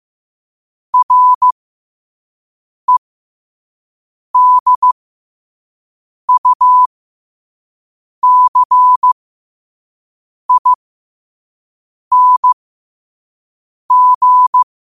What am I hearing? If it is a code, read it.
Morse code, heard as a keyed tone, about 15 wpm, spaced as for 5 wpm, 1000 Hz. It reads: REDUCING
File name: qotdfiles / 5wpm